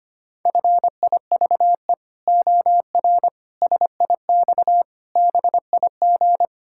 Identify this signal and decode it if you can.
Morse code, heard as a keyed tone, about 25 wpm, 700 Hz. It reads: FIVE OR SIX BIG
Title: Pangram Challenge v2 - 25wpm